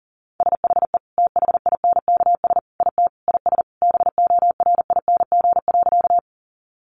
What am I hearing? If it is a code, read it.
Morse code, heard as a keyed tone, about 37 wpm, 700 Hz. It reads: SHE THINKS IT IS BORING.
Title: Easy Phrases 40wpm